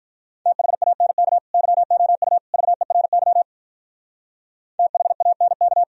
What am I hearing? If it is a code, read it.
Morse code, heard as a keyed tone, about 46 wpm, 700 Hz. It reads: THANK YOU VERY THANK